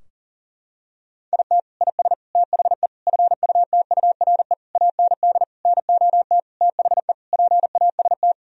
Morse code, heard as a keyed tone, about 37 wpm, 700 Hz. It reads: IT IS THE FUTURE AND NOT THE PAST
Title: Sentences from Top 1000 Words - Review Part 4 - 40wpm